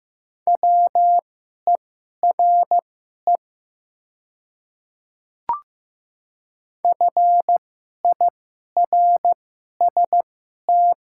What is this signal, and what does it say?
Morse code, heard as a keyed tone, about 15 wpm, spaced as for 11 wpm, 700 Hz. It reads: WERE FIRST
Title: 2x Character Spacing - Top 200 Words - 15wpm